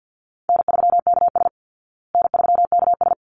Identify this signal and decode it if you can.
Morse code, heard as a keyed tone, about 50 wpm, 700 Hz. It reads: N3KS N3KS
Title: Call Signs Repeated 3x - 50wpm